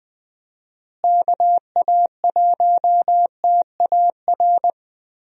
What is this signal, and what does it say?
Morse code, heard as a keyed tone, about 20 wpm, 700 Hz. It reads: KA1TAR
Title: USA Call Signs - 6 Characters - Repeated 2x - 20wpm